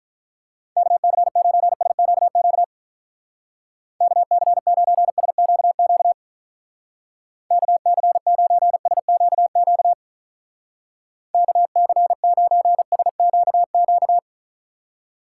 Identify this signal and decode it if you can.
Morse code, heard as a keyed tone, about 44 wpm, 700 Hz. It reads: KC9SQQ KC9SQQ KC9SQQ KC9SQQ
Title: Speed Racing - USA 2x3 Call Signs 35wpm